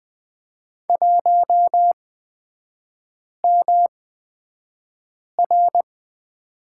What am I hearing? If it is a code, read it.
Morse code, heard as a keyed tone, about 20 wpm, spaced as for 5 wpm, 690 Hz. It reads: 1MR
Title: 229 Lesson 37e - Letter x - Three Characters 20wpm